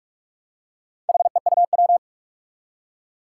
Morse code, heard as a keyed tone, about 45 wpm, 700 Hz. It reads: HEUW
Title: Four Challenging Letter-Number Combinations 45wpm